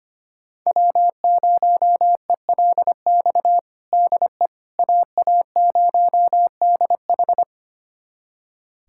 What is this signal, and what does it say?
Morse code, heard as a keyed tone, about 25 wpm, 700 Hz. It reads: W0ELX DE AA0DH
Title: USA Callsign DE Callsign - 5 Characters - 25wpm